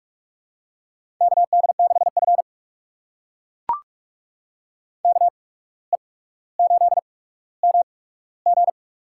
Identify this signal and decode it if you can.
Morse code, heard as a keyed tone, about 44 wpm, spaced as for 18 wpm, 690 Hz. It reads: KD6F KE8MG
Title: USA Call Signs - 8x Character Spacing - 45wpm